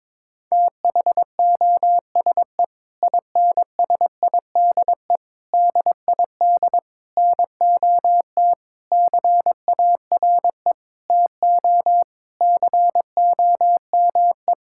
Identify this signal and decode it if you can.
Morse code, heard as a keyed tone, about 22 wpm, 700 Hz. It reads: THOSE INSIDE DID NOT CARE TO COME
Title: Sentences from Top 1000 Words - Review Part 6 - 22wpm